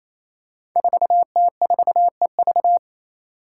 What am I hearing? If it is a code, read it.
Morse code, heard as a keyed tone, about 28 wpm, 700 Hz. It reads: 4T4EV
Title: Five Challenging Letter-Number Combinations 28wpm